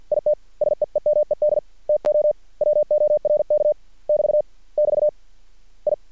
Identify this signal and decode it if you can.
Morse code, heard as a keyed tone, about 50 wpm, 600 Hz. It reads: IT SEEMED TO WORK <BT> <BT> I
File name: archived / CRSnet